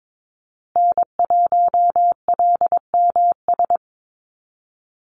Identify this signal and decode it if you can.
Morse code, heard as a keyed tone, about 22 wpm, 700 Hz. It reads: N1LMS
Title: USA Call Signs Repeated 2x 22wpm